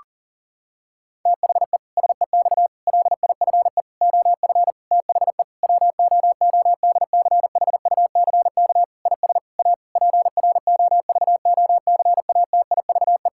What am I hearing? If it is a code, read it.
Morse code, heard as a keyed tone, about 37 wpm, 700 Hz. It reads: THE SEX LIFE OF THE WOODCHUCK IS A PROVOCATIVE
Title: Pangram Challenge v2 - 40wpm